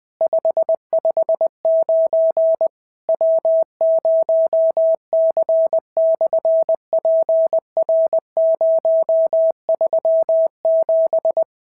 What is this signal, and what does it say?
Morse code, heard as a keyed tone, about 20 wpm, 650 Hz. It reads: 559 W0C/PR037
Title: SOTA QSO Practice 20wpm